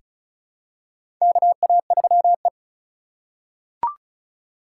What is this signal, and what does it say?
Morse code, heard as a keyed tone, about 36 wpm, 700 Hz. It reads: KA3E
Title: USA Call Signs - 4 Characters - 4x Character Spacing - 35wpm